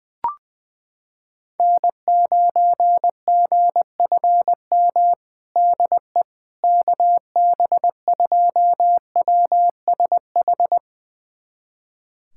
Morse code, heard as a keyed tone, about 20 wpm, 700 Hz. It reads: N9GFM DE KB2WSH
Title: USA Callsign DE Callsign 20wpm